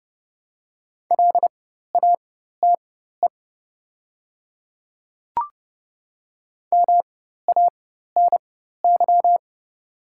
Morse code, heard as a keyed tone, about 30 wpm, spaced as for 14 wpm, 690 Hz. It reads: LATE MANY
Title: 4x Character Spacing - Top 300 Words - 30wpm